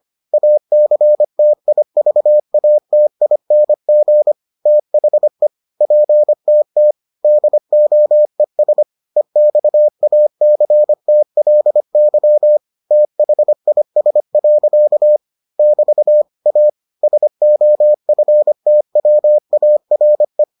Morse code, heard as a keyed tone, about 25 wpm, 600 Hz. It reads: ACTIVATING THE PTT DOES EXACTLY THIS. <BT> A SOFTWARE